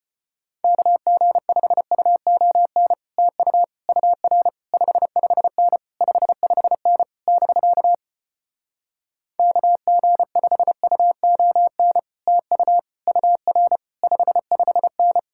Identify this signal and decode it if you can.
Morse code, heard as a keyed tone, about 33 wpm, 700 Hz. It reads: KG5UON TU UR 55N 55N <BK> KG5UON TU UR 55N